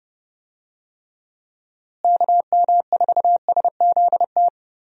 Morse code, heard as a keyed tone, about 30 wpm, 700 Hz. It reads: KM4SZT
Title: USA Call Signs - 6 Characters - Repeated 3x - 30wpm